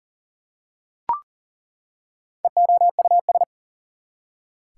Morse code, heard as a keyed tone, about 40 wpm, 700 Hz. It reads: EOUS
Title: Prefixes and Suffixes 40wpm